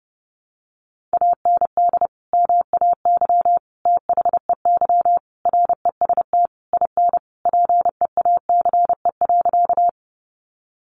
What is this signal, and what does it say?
Morse code, heard as a keyed tone, about 30 wpm, 700 Hz. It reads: AND MAY THEY REST IN PEACE.
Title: Sentences from Top 1500 Words 30wpm